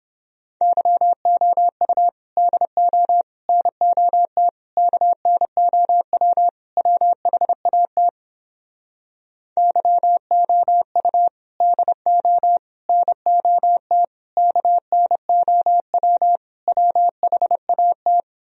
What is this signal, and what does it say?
Morse code, heard as a keyed tone, about 31 wpm, 700 Hz. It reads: YOU DO NOT KNOW WHAT YOU DO NOT KNOW WHAT